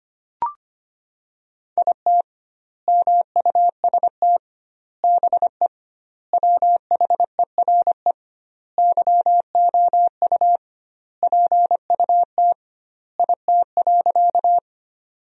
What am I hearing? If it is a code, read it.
Morse code, heard as a keyed tone, about 25 wpm, 700 Hz. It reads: IT MUST BE WHERE YOU PUT IT.
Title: 2x Word-Spacing - Sentences from Top 200 Words - 25wpm